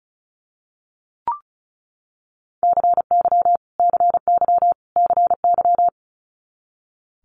Morse code, heard as a keyed tone, about 36 wpm, 700 Hz. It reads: CY CY CY
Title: Multi-letter Phonemes Repeated 3x 35wpm